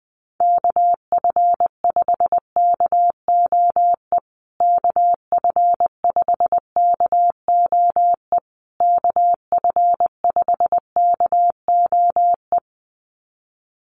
Morse code, heard as a keyed tone, about 20 wpm, 700 Hz. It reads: KF5KOE KF5KOE KF5KOE
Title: USA Call Signs Repeated 3x 20wpm